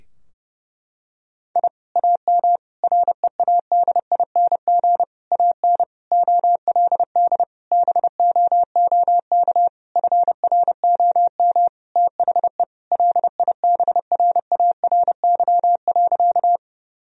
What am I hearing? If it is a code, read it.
Morse code, heard as a keyed tone, about 32 wpm, 700 Hz. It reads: I AM READING AN OLD BOOK FROM THE LIBRARY.
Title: Sentences from Top 1000 Words - Review Part 4 - 30wpm